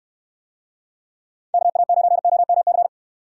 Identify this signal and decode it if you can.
Morse code, heard as a keyed tone, about 62 wpm, 700 Hz. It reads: KA9OMC